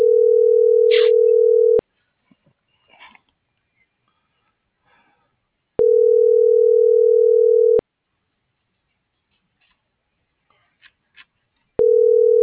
Ringback tone (North America), 440 + 480 Hz.